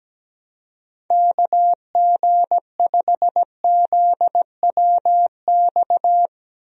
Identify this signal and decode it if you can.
Morse code, heard as a keyed tone, about 17 wpm, 700 Hz. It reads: KG5ZWX
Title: USA Call Signs - 6 Characters - Repeated 2x - 17wpm